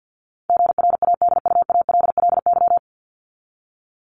Morse code, heard as a keyed tone, about 50 wpm, 700 Hz. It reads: GRADUALLY
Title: LY Suffix - Top 100 Words with Base Words 50wpm